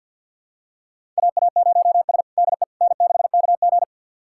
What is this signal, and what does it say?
Morse code, heard as a keyed tone, about 52 wpm, 700 Hz. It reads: AA0S DE N6KG
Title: USA Callsign DE Callsign - 4 Characters - 50wpm